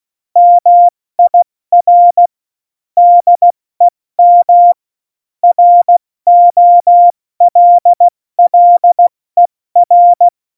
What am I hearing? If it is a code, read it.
Morse code, heard as a keyed tone, about 16 wpm, 700 Hz. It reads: MIR DEM ROLLER
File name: Morsekurs / Witze / Witz 29. August 17, Tempo 70 F80